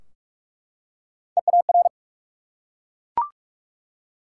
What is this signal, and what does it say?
Morse code, heard as a keyed tone, about 44 wpm, 700 Hz. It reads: EAR
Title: Three Letter Words - 45wpm